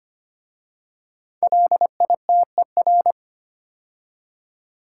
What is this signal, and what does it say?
Morse code, heard as a keyed tone, about 25 wpm, 700 Hz. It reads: LITER
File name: 5 Letter Words - First 3 Letters Given - 25wpm